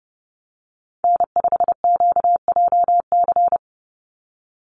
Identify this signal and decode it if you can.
Morse code, heard as a keyed tone, about 30 wpm, 700 Hz. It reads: N5QJC
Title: USA Call Signs 30wpm